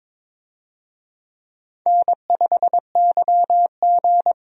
Morse code, heard as a keyed tone, about 21 wpm, 700 Hz. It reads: N5YG
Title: USA Call Signs - 4 Characters - 22wpm